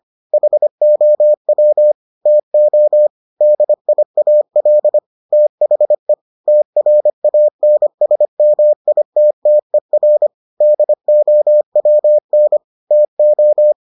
Morse code, heard as a keyed tone, about 25 wpm, 600 Hz. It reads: HOW TO DIAL THE TRANSMITTER DOWN TO